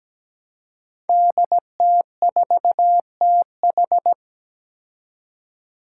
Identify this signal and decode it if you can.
Morse code, heard as a keyed tone, about 17 wpm, 700 Hz. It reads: DT4TH
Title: Five Challenging Letter-Number Combinations 17wpm